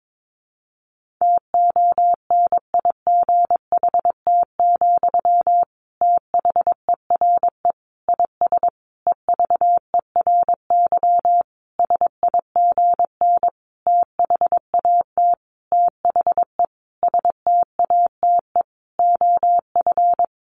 Morse code, heard as a keyed tone, about 22 wpm, 700 Hz. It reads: TONIGHT, THERE IS EVERY SIGN THAT THE STATE OF
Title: Sentences from Top 1400 Words 22wpm